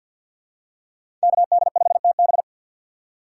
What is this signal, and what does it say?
Morse code, heard as a keyed tone, about 50 wpm, 700 Hz. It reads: KD5TB
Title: USA Call Signs - 5 Characters - 2x Character Spacing - 50wpm